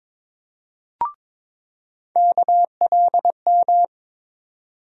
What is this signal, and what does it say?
Morse code, heard as a keyed tone, about 22 wpm, 700 Hz. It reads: KLM
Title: USA Call Sign Postfixes - 3 Letters 22wpm